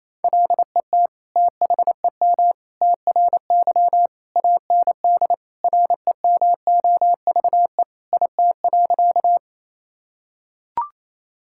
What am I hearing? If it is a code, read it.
Morse code, heard as a keyed tone, about 28 wpm, 700 Hz. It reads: LET THEM TRY AND REMOVE IT.
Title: Sentences from Top 1000 Words - Review Part 5 - 28wpm